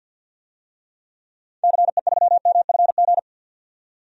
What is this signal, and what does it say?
Morse code, heard as a keyed tone, about 52 wpm, 700 Hz. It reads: CE3MFG
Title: DXCC-100 Call Signs - 6 Characters - 50wpm